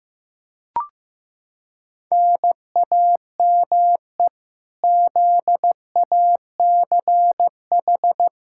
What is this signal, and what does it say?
Morse code, heard as a keyed tone, about 15 wpm, 700 Hz. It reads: NAME ZACH
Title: Practice QSO - Exchange 1 - Beginning - 15wpm